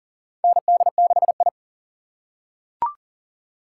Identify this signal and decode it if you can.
Morse code, heard as a keyed tone, about 40 wpm, 700 Hz. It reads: ND6I
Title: USA Call Signs - 4 Characters - 6x Character Spacing - 40wpm